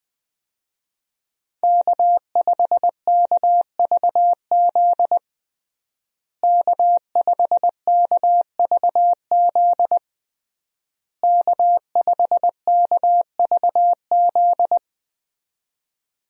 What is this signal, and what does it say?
Morse code, heard as a keyed tone, about 21 wpm, 700 Hz. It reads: K5KVZ K5KVZ K5KVZ
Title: USA Call Signs - 5 Characters - Repeated 3x - 20wpm